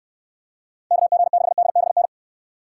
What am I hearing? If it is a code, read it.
Morse code, heard as a keyed tone, about 63 wpm, 700 Hz. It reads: KG6DBN